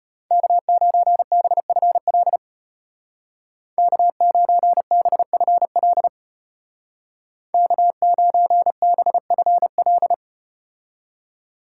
Morse code, heard as a keyed tone, about 34 wpm, 700 Hz. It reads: K9BFL K9BFL K9BFL